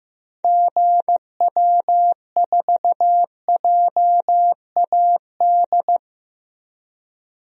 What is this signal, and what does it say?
Morse code, heard as a keyed tone, about 16 wpm, 700 Hz. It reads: GW4JAD
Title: DXCC-100 Call Signs - 6 Characters - 15wpm